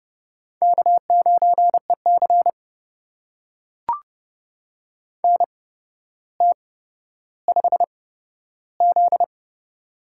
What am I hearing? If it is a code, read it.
Morse code, heard as a keyed tone, about 31 wpm, spaced as for 13 wpm, 700 Hz. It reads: K9EC NT5Z
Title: USA Call Signs - 4 Characters - 8x Character Spacing - 30wpm